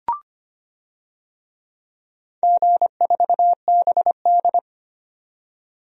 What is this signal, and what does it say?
Morse code, heard as a keyed tone, about 25 wpm, 700 Hz. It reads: G4BD
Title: DXCC-100 Call Signs - 4 Characters - 25wpm